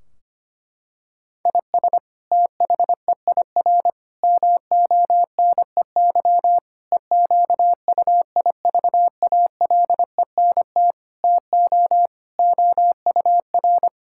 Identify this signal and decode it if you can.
Morse code, heard as a keyed tone, about 25 wpm, 700 Hz. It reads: IS THEIR MONEY EQUIVALENT TO OUR
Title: Sentences from Top 1600 Words 25wpm